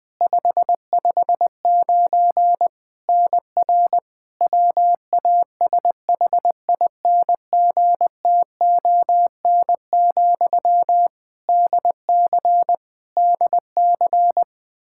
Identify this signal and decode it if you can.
Morse code, heard as a keyed tone, about 20 wpm, 700 Hz. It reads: 559 NR WASHINGTON, DC DC
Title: Practice QSO - Exchange 1 - Beginning - 20wpm